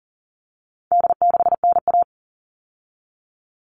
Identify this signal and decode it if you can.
Morse code, heard as a keyed tone, about 38 wpm, 700 Hz. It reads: D6NA
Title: All Cut Numbers - 4-Digit 40wpm